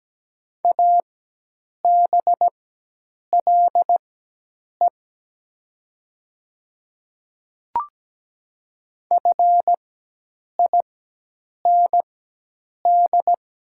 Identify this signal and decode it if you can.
Morse code, heard as a keyed tone, about 17 wpm, spaced as for 8 wpm, 700 Hz. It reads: ABLE FIND
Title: 4x Character Spacing - Top 500 Words - 17wpm